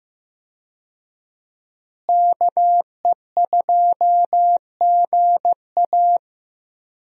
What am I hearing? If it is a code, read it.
Morse code, heard as a keyed tone, about 15 wpm, 700 Hz. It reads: KE2GA
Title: USA Call Signs - 5 Characters - Repeated 3x - 15wpm